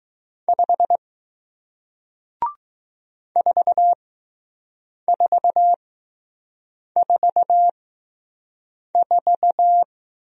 Morse code, heard as a keyed tone, about 20 wpm, spaced as for 6 wpm, 700 Hz. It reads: 5 4444